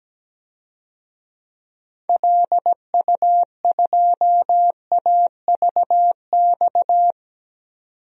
Morse code, heard as a keyed tone, about 17 wpm, 700 Hz. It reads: LU2AVX
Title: DXCC-100 Call Signs - 6 Characters - 17wpm